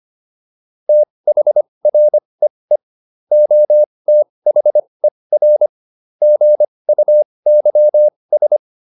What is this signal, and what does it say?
Morse code, heard as a keyed tone, about 25 wpm, spaced as for 20 wpm, 600 Hz. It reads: THREE OTHER GUYS